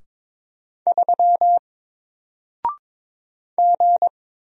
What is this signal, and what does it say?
Morse code, heard as a keyed tone, about 22 wpm, 700 Hz. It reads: 3 G
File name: Single-Letter-Number-22wpm